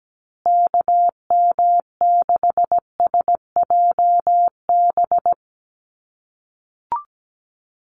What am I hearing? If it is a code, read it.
Morse code, heard as a keyed tone, about 17 wpm, 700 Hz. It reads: KM6SJB
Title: USA Call Signs Repeated 3x 17wpm